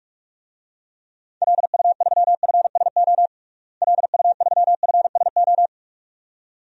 Morse code, heard as a keyed tone, about 46 wpm, 700 Hz. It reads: LU3FSO LU3FSO
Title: DXCC-100 Call Signs - 6 Characters - Repeated 2x - 45wpm